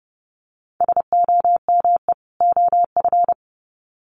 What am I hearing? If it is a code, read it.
Morse code, heard as a keyed tone, about 30 wpm, 700 Hz. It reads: SOME OF
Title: Sets of 2 Words 30wpm